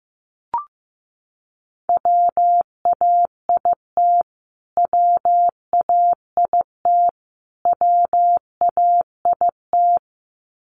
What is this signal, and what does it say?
Morse code, heard as a keyed tone, about 15 wpm, 700 Hz. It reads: WAIT WAIT WAIT
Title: Top 500 Words Repeated 15wpm